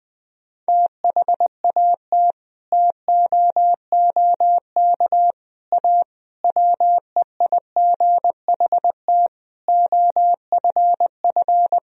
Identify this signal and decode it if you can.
Morse code, heard as a keyed tone, about 20 wpm, 700 Hz. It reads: THAT TOOK A WEIGHT OFF
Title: Sentences from Top 1000 Words - Review Part 3 - 20wpm